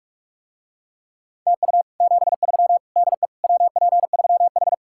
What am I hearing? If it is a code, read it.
Morse code, heard as a keyed tone, about 46 wpm, 700 Hz. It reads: TU 73 DE WP3H